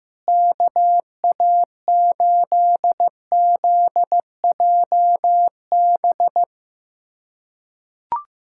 Morse code, heard as a keyed tone, about 15 wpm, 700 Hz. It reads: KA8ZJB
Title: USA Callsign DE Callsign - 6 Characters - 15wpm